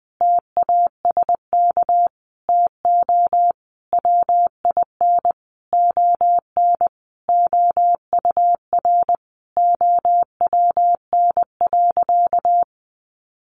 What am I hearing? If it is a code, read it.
Morse code, heard as a keyed tone, about 20 wpm, 700 Hz. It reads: TASK TO WIN ON OUR OWN.
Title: Sentences from Top 1500 Words 20wpm